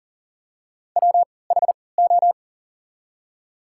Morse code, heard as a keyed tone, about 40 wpm, spaced as for 23 wpm, 700 Hz. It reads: WHO
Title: 3x Character Spacing - Top 500 Words - 40wpm